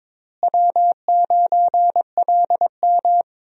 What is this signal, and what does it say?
Morse code, heard as a keyed tone, about 22 wpm, 700 Hz. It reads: W9LM
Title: USA Call Signs - 4 Characters - Repeated 2x - 22wpm